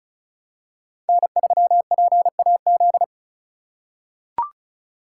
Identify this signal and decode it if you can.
Morse code, heard as a keyed tone, about 36 wpm, 700 Hz. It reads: N3PAZ
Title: USA Call Signs - 5 Characters - 4x Character Spacing - 35wpm